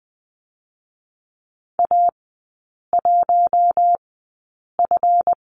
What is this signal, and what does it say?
Morse code, heard as a keyed tone, about 20 wpm, spaced as for 8 wpm, 700 Hz. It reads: A1F
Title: 152 Lesson 25e - Letter p - Three Characters 20wpm